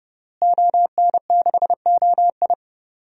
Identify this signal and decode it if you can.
Morse code, heard as a keyed tone, about 30 wpm, 700 Hz. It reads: ON6OI
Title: DXCC-100 Call Signs - 5 Characters - Repeated 2x - 30wpm